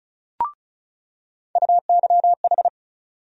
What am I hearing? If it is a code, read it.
Morse code, heard as a keyed tone, about 36 wpm, 700 Hz. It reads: UYH
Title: USA Call Sign Postfixes - 3 Letters 35wpm